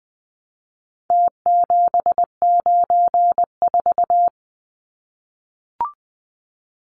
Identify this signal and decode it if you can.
Morse code, heard as a keyed tone, about 20 wpm, 700 Hz. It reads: T794
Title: Numbers - Contest Style - 4-Digit 20wpm